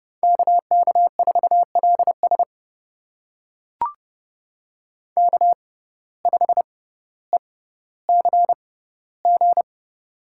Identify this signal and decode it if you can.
Morse code, heard as a keyed tone, about 32 wpm, spaced as for 16 wpm, 700 Hz. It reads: KK4LS K5ECG
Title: USA Call Signs - 5 Characters - 6x Character Spacing - 30wpm